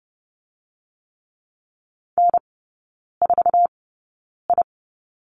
Morse code, heard as a keyed tone, about 30 wpm, spaced as for 9 wpm, 690 Hz. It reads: N4I
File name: Lesson 16e - Letter c - Three Characters 30wpm